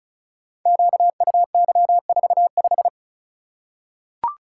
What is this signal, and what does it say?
Morse code, heard as a keyed tone, about 34 wpm, 700 Hz. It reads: QUY45
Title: Code Groups - 5 Letter:Numbers - 35wpm